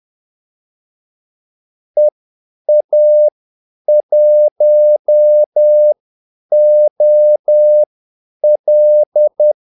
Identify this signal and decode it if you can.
Morse code, heard as a keyed tone, about 10 wpm, spaced as for 8 wpm, 600 Hz. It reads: EA1OL
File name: Callsigns-10-8wpm